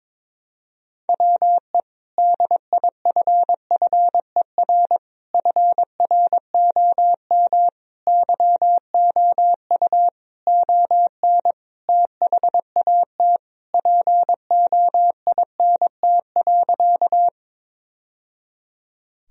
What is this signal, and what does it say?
Morse code, heard as a keyed tone, about 22 wpm, 700 Hz. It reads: WE DIFFER FROM YOU ON THAT POINT.
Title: Sentences from Top 1000 Words - Review Part 1 - 22wpm